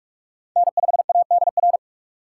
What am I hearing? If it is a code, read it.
Morse code, heard as a keyed tone, about 46 wpm, 700 Hz. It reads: N5ADR
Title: USA Call Signs - 2x Character Spacing - 45wpm